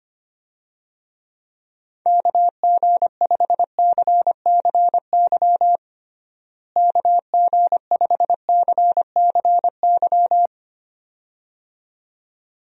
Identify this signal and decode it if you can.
Morse code, heard as a keyed tone, about 25 wpm, 700 Hz. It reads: KG5CCY KG5CCY
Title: USA Call Signs - 6 Characters - Repeated 2x - 25wpm